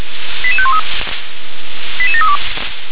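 Five-tone selective call (ZVEI), heard as quick runs of five tones, tones of about 70 ms. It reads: 88513 88732
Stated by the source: ZVEI five-tone selective call received by SDR.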